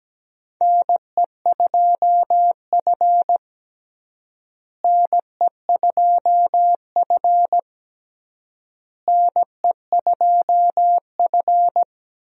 Morse code, heard as a keyed tone, about 17 wpm, 700 Hz. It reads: NE2F NE2F NE2F
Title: USA Call Signs - 4 Characters - Repeated 3x - 17wpm